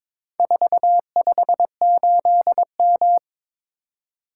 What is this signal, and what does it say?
Morse code, heard as a keyed tone, about 22 wpm, 700 Hz. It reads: 458M
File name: practice / 22wpm / Four Letter-Number Combinations - 22wpm